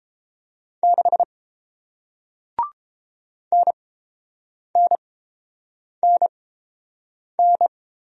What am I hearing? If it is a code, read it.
Morse code, heard as a keyed tone, about 30 wpm, spaced as for 7 wpm, 700 Hz. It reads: 6 NNNN